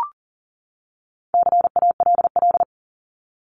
Morse code, heard as a keyed tone, about 40 wpm, 700 Hz. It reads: CALL
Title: Top 300 Words 40wpm